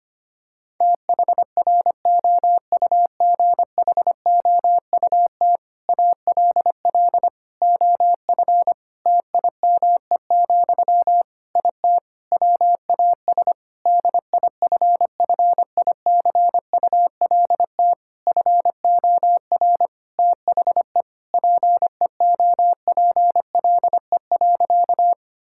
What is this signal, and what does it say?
Morse code, heard as a keyed tone, about 25 wpm, 700 Hz. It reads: THROUGHOUT ALL OF TIME, IT WAS DIFFICULT FOR THE PEOPLE.
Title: Sentences from Top 1000 Words - Review Part 5 - 25wpm